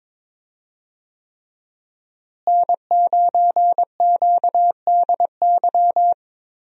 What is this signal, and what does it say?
Morse code, heard as a keyed tone, about 22 wpm, 700 Hz. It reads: N9QDY
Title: USA Call Signs - 5 Characters - Repeated 3x - 22wpm